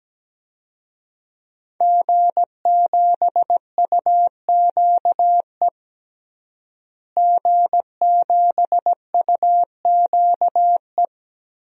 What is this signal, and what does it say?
Morse code, heard as a keyed tone, about 17 wpm, 700 Hz. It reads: G7UQE G7UQE
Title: DXCC-100 Call Signs - 5 Characters - Repeated 2x - 17wpm